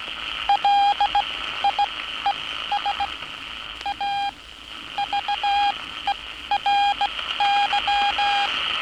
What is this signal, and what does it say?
Morse code, heard as a keyed tone, about 17 wpm, spaced as for 13 wpm, 810 Hz, with noise. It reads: LIES A VERY
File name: training / NAVPERS 11049to11060 / NAVPERS 11059 RA RB